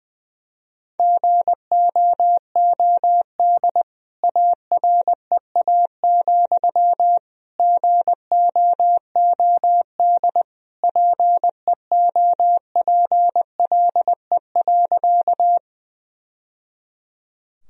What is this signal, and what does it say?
Morse code, heard as a keyed tone, about 21 wpm, 700 Hz. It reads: GOOD AREA, GOOD PEOPLE.
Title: Sentences from Top 1000 Words - Review Part 2 - 20wpm